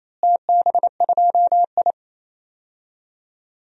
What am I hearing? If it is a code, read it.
Morse code, heard as a keyed tone, about 28 wpm, 700 Hz. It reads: TB2I
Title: DXCC-100 Call Signs - 4 Characters - Repeated 3x - 28wpm